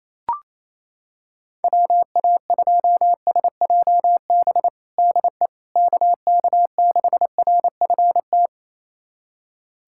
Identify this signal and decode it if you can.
Morse code, heard as a keyed tone, about 28 wpm, 700 Hz. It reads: WA2SJB DE KK6RFT
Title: USA Callsign DE Callsign - 6 Characters - 28wpm